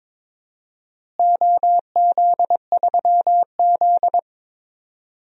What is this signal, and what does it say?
Morse code, heard as a keyed tone, about 22 wpm, 700 Hz. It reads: OZ3Z
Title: DXCC-100 Call Signs - 4 Characters - Repeated 3x - 22wpm